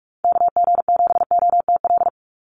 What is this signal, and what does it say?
Morse code, heard as a keyed tone, about 46 wpm, 700 Hz. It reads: KG7OTL
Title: USA Call Signs 45wpm